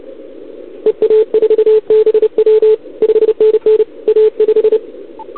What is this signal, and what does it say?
Morse code, heard as a keyed tone, about 29 wpm, 440 Hz. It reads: EA4BW 5NN A5
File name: CQWWCW2021 / OL3Z